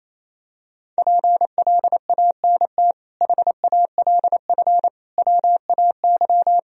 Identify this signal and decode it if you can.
Morse code, heard as a keyed tone, about 28 wpm, 700 Hz. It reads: PLANT HALF WAY
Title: Sentences from Top 1000 Words - Review Part 1 - 28wpm